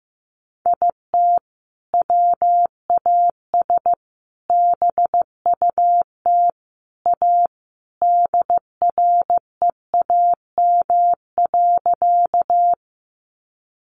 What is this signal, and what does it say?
Morse code, heard as a keyed tone, about 15 wpm, 700 Hz. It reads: IT WAS BUT A DREAM.
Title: Sentences from Top 1500 Words 15wpm